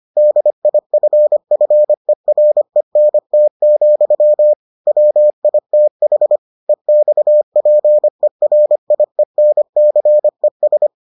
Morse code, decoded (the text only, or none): DIFFERENT, WITH EXPERIENCES